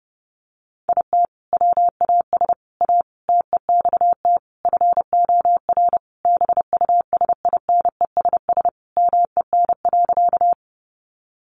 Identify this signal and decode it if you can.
Morse code, heard as a keyed tone, about 30 wpm, 700 Hz. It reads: IT WAS A TEXT FOR BUSINESS MEN.
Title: Sentences from Top 1200 Words 30wpm